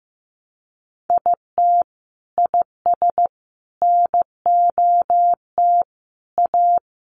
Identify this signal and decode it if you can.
Morse code, heard as a keyed tone, about 15 wpm, 700 Hz. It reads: IT IS NOT A
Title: Sentences from Top 800 Words 15wpm